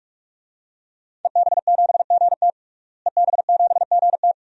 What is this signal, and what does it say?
Morse code, heard as a keyed tone, about 46 wpm, 700 Hz. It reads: EB7GT EB7GT
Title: DXCC-100 Call Signs - 5 Characters - Repeated 2x - 45wpm